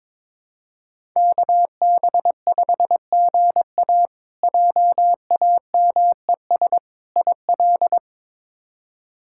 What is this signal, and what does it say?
Morse code, heard as a keyed tone, about 21 wpm, 700 Hz. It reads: KB5GA JAMES IL
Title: K1USN SST QSO Practice - 22wpm